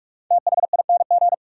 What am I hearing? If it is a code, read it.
Morse code, heard as a keyed tone, about 46 wpm, 700 Hz. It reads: THING
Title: Sets of 3 Words - New Word Drill - 45wpm